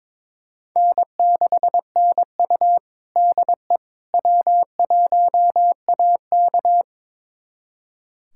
Morse code, heard as a keyed tone, about 22 wpm, 700 Hz. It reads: N6NU DE W1AK
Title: USA Callsign DE Callsign - 4 Characters - 22wpm